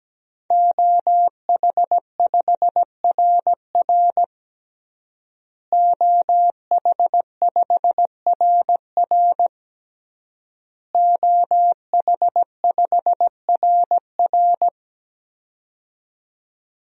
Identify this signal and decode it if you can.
Morse code, heard as a keyed tone, about 17 wpm, 700 Hz. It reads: OH5RR OH5RR OH5RR
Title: DXCC-100 Call Signs - 5 Characters - Repeated 3x - 17wpm